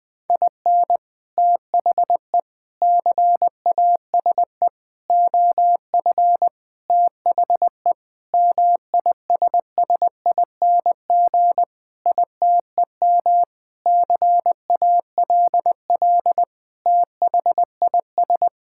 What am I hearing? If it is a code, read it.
Morse code, heard as a keyed tone, about 21 wpm, 700 Hz. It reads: IN THE CASE OF THE MISSING ITEM CALL THIS
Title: Sentences with 5-Word Phrases - 20wpm